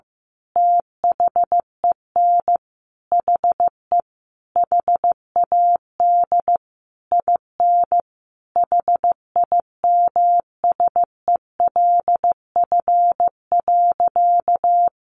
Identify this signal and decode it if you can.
Morse code, heard as a keyed tone, about 15 wpm, 700 Hz. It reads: THEN HE HAD IN HIMSELF.
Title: Sentences from Top 1500 Words 15wpm